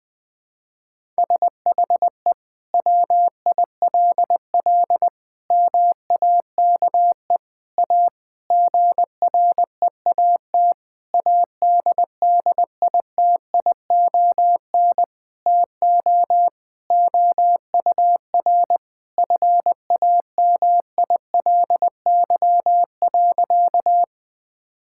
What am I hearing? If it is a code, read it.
Morse code, heard as a keyed tone, about 20 wpm, 700 Hz. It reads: SHE WILL MAKE A GREAT ADDITION TO OUR FAMILY.
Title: Sentences from Top 1000 Words - Review Part 6 - 20wpm